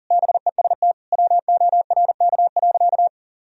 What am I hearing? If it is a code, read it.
Morse code, heard as a keyed tone, about 40 wpm, 700 Hz. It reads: BEST WORK.
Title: Sentences from Top 1000 Words - Review Part 3 - 40wpm